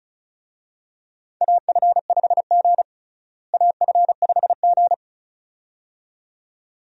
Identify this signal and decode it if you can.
Morse code, heard as a keyed tone, about 34 wpm, 700 Hz. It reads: AF5G AF5G
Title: USA Call Signs - 4 Characters - Repeated 2x - 35wpm